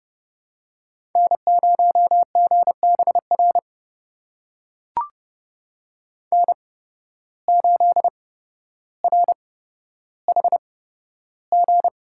Morse code, heard as a keyed tone, about 30 wpm, spaced as for 13 wpm, 700 Hz. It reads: N0GBR N8RHG
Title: USA Call Signs - 5 Characters - 8x Character Spacing - 30wpm